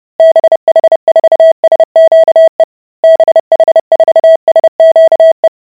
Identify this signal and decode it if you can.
Morse code, heard as a keyed tone, about 30 wpm, 650 Hz. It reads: BH4SQE BH4SQE